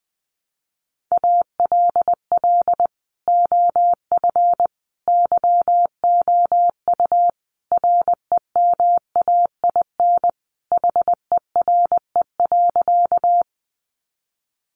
Morse code, heard as a keyed tone, about 20 wpm, 700 Hz. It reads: ALL OF YOU REMAIN HERE.
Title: Sentences from Top 1200 Words 20wpm